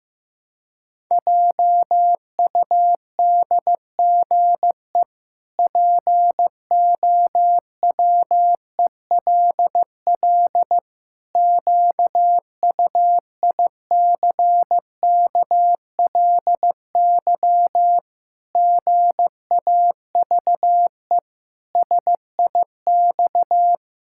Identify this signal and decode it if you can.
Morse code, heard as a keyed tone, about 15 wpm, 700 Hz. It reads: JUDGE POWELL QUICKLY GAVE SIX
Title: Pangram Challenge v2 - 15wpm